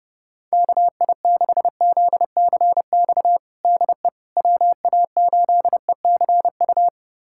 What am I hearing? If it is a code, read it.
Morse code, heard as a keyed tone, about 30 wpm, 700 Hz. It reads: KI6ZCX DE WA8ECU
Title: USA Callsign DE Callsign - 6 Characters - 30wpm